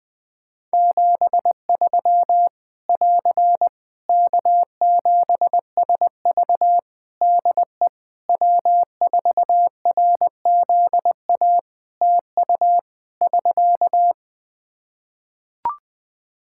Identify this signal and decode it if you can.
Morse code, heard as a keyed tone, about 20 wpm, 700 Hz. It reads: 73 <AR> K7SV DE W4RZA TU <SK>
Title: Practice QSO - Exchange 4 - Ending - 20wpm